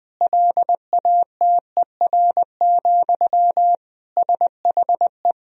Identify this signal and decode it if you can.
Morse code, heard as a keyed tone, about 20 wpm, 700 Hz. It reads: LATER, SHE
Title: Sentences from Top 1000 Words - Review Part 4 - 20wpm